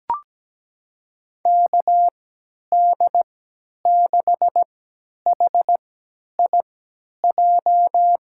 Morse code, heard as a keyed tone, about 17 wpm, spaced as for 10 wpm, 700 Hz. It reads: KD6HIJ
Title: USA Call Signs - 3x Character Spacing - 17wpm